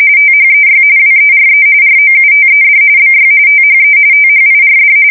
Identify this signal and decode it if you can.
RTTY, about 45.45 baud. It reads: THEY WERE ROUTED BY THE THIRD
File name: fsk170 / USS-New-Jersey